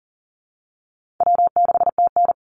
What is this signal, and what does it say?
Morse code, heard as a keyed tone, about 40 wpm, 700 Hz. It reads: W6TN
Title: Easy Call Signs Repeated 2x - 40wpm